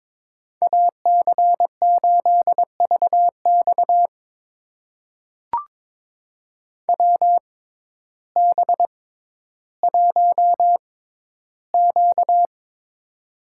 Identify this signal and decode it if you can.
Morse code, heard as a keyed tone, about 21 wpm, spaced as for 12 wpm, 700 Hz. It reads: AC8VX WB1Q
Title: USA Call Signs - 6x Character Spacing - 22wpm